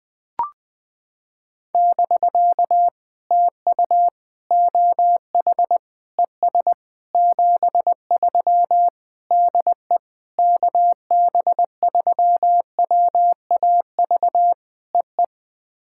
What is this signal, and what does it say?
Morse code, heard as a keyed tone, about 20 wpm, 700 Hz. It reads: <BK> TU OH ES 73 DE KB3WAV EE
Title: POTA QSO Practice - 20wpm